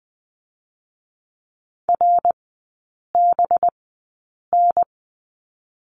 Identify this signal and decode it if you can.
Morse code, heard as a keyed tone, about 20 wpm, spaced as for 8 wpm, 700 Hz. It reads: RBN
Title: 184 Lesson 30e - Letter b - Three Characters 20wpm